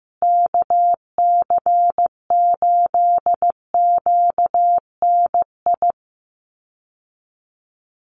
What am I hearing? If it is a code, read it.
Morse code, heard as a keyed tone, about 15 wpm, 700 Hz. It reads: KC8QNI
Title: USA Call Signs Repeated 2x 15wpm